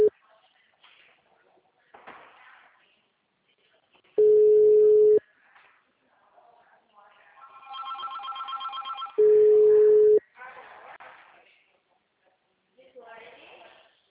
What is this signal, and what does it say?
Ringback tone (Europe, 425 Hz), with speech.